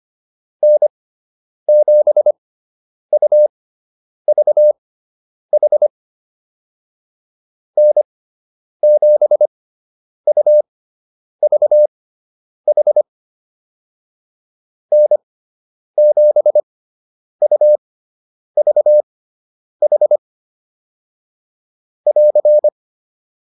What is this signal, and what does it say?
Morse code, heard as a keyed tone, about 25 wpm, spaced as for 9 wpm, 600 Hz. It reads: N7UVH N7UVH N7UVH <AR>
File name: QSO 105 09